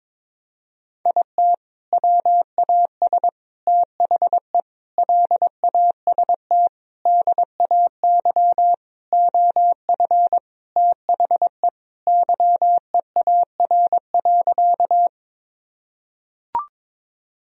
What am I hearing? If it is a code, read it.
Morse code, heard as a keyed tone, about 23 wpm, 700 Hz. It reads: IT WAS THE LAST DAY OF THE YEAR.
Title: Sentences from Top 1000 Words - Review Part 1 - 22wpm